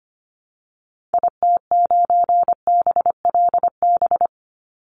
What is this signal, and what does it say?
Morse code, heard as a keyed tone, about 25 wpm, 700 Hz. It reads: IT9BLB
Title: Call Signs Repeated 3x - 25wpm